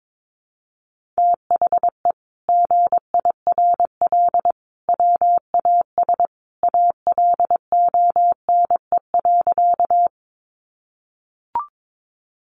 Morse code, heard as a keyed tone, about 22 wpm, 700 Hz. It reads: THE GIRL WAS ALONE.
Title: Sentences from Top 1000 Words 22wpm